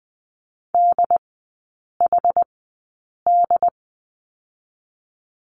Morse code, heard as a keyed tone, about 20 wpm, spaced as for 8 wpm, 690 Hz. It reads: DHD
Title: 087 Lesson 15e - Number 5 - Three Characters 20wpm